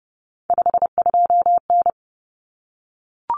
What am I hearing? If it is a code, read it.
Morse code, heard as a keyed tone, about 30 wpm, 700 Hz. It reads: 52N
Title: Common Cut Numbers - 3-Digit 30wpm